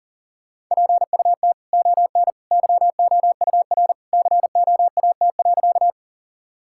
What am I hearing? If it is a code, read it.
Morse code, heard as a keyed tone, about 37 wpm, 700 Hz. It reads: PUT ON YOUR COAT.
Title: Sentences from Top 1600 Words 40wpm